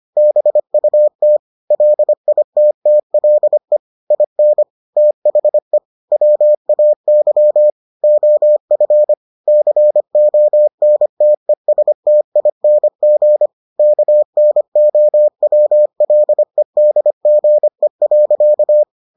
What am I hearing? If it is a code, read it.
Morse code, heard as a keyed tone, about 25 wpm, 600 Hz. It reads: BUT LITTLE IN THE WAY OF CONTESTING KNOWLEDGE.